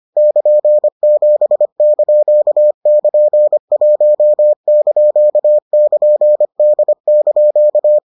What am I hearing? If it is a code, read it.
Morse code, heard as a keyed tone, about 25 wpm, 600 Hz. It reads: <KN>7)<KN>1)<KN>D)